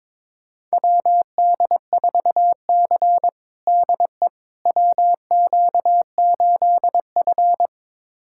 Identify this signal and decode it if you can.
Morse code, heard as a keyed tone, about 22 wpm, 700 Hz. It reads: WD4C DE WQ8F
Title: USA Callsign DE Callsign - 4 Characters - 22wpm